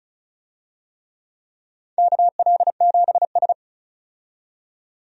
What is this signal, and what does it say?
Morse code, heard as a keyed tone, about 36 wpm, 700 Hz. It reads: KL7S
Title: DXCC-100 Call Signs - 4 Characters - 35wpm